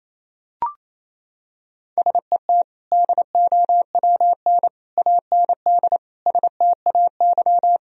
Morse code, heard as a keyed tone, about 28 wpm, 700 Hz. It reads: SET DOWN AND STAY
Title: Sentences from Top 1000 Words - Review Part 1 - 28wpm